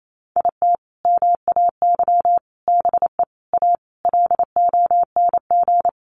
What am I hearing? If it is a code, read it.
Morse code, heard as a keyed tone, about 28 wpm, 700 Hz. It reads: IT MAY BE A LONG
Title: Sentences from Top 1100 Words 28wpm